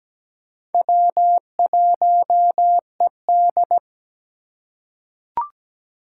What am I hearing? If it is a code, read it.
Morse code, heard as a keyed tone, about 17 wpm, 700 Hz. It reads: W1ED
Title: USA Call Signs - 4 Characters - 3x Character Spacing - 17wpm